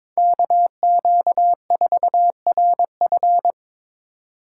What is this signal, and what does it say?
Morse code, heard as a keyed tone, about 21 wpm, 700 Hz. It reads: KQ4RF
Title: USA Call Signs - 5 Characters - 4x Character Spacing - 22wpm